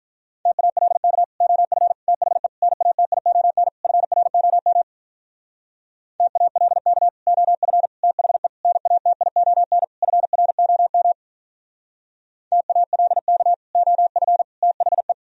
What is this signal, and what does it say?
Morse code, heard as a keyed tone, about 44 wpm, 700 Hz. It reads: TALK OF THE NATION FROM TALK OF THE NATION FROM TALK OF THE
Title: Speed-Racing - Sets of 5 Words 35wpm